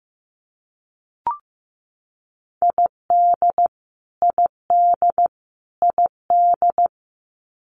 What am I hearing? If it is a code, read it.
Morse code, heard as a keyed tone, about 15 wpm, 700 Hz. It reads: ID ID ID
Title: US States Repeated 3x 15wpm